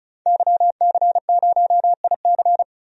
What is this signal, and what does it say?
Morse code, heard as a keyed tone, about 34 wpm, 700 Hz. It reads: YC0IC
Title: DXCC-100 Call Signs - 5 Characters - 35wpm